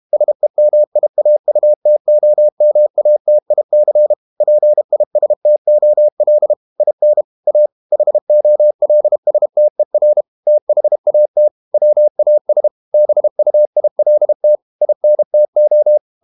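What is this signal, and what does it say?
Morse code, heard as a keyed tone, about 32 wpm, 600 Hz. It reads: SEMIAUTOMATIC PISTOL IN A HOLSTER THAT WAS BUILT INTO
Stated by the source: Morse code practice at 32 wpm from CW med Gnister, reading Ken Follett's novel Never in English.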